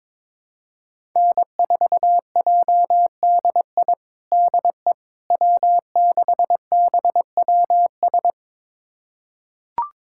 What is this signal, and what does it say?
Morse code, heard as a keyed tone, about 21 wpm, 700 Hz. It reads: N4JDI DE W6BWS
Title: USA Callsign DE Callsign - 5 Characters - 22wpm